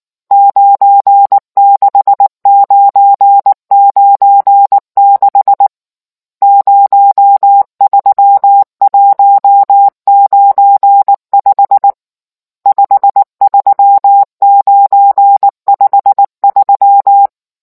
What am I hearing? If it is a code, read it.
Morse code, heard as a keyed tone, about 19 wpm, 800 Hz. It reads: 96996 03195 53953